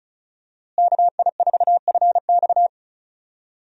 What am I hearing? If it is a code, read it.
Morse code, heard as a keyed tone, about 36 wpm, 700 Hz. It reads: KI4FX
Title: USA Call Signs - 5 Characters - Repeated 3x - 35wpm